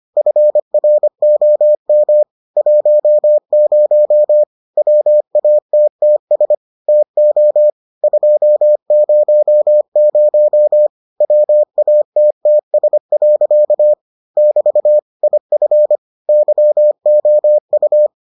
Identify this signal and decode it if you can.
Morse code, heard as a keyed tone, about 25 wpm, 600 Hz. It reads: FROM 10 WATTS TO 200 WATTS. <BT> IF YOU